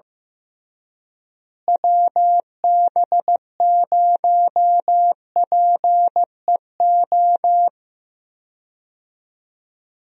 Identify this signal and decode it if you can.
Morse code, heard as a keyed tone, about 15 wpm, 700 Hz. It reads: WB0PEO
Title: USA Call Signs - 6 Characters - Repeated 2x - 15wpm